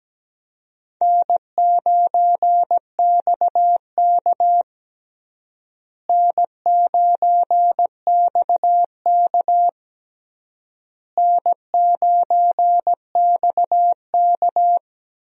Morse code, heard as a keyed tone, about 17 wpm, 700 Hz. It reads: N9XK N9XK N9XK
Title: USA Call Signs - 4 Characters - Repeated 3x - 17wpm